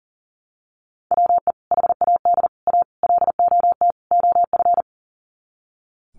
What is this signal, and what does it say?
Morse code, heard as a keyed tone, about 40 wpm, 700 Hz. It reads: WE HAD A LOT OF
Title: Sets of 5 Words Encore 40wpm